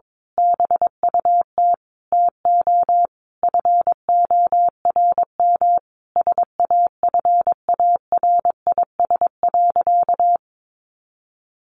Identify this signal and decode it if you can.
Morse code, heard as a keyed tone, about 22 wpm, 700 Hz. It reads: BUT TO FORM SAFARIS.
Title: A Sound Of Thunder 22wpm